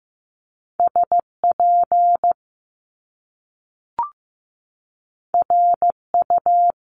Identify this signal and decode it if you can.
Morse code, heard as a keyed tone, about 15 wpm, 700 Hz. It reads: SP RU
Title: Top Two Letter Combinations in Words 15wpm